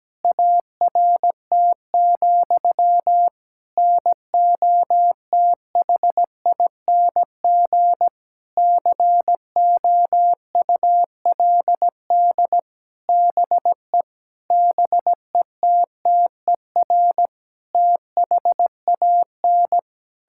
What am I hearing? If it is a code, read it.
Morse code, heard as a keyed tone, about 17 wpm, 700 Hz. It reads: ART, NOTHING COULD BE BETTER THAN
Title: Sentences from Top 1000 Words - Review Part 6 - 17wpm